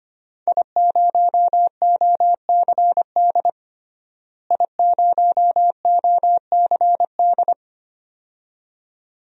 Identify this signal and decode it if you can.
Morse code, heard as a keyed tone, about 25 wpm, 700 Hz. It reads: I0OCD I0OCD
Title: DXCC-100 Call Signs - 5 Characters - Repeated 2x - 25wpm